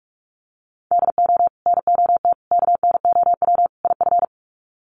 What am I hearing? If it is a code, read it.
Morse code, heard as a keyed tone, about 44 wpm, 700 Hz. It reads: DO NOT KNOW IF
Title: Sets of 4 Words 45wpm